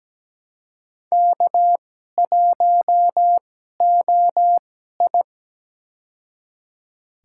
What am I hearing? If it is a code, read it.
Morse code, heard as a keyed tone, about 17 wpm, spaced as for 12 wpm, 700 Hz. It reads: K1OI
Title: USA Call Signs - 2x Character Spacing - 17wpm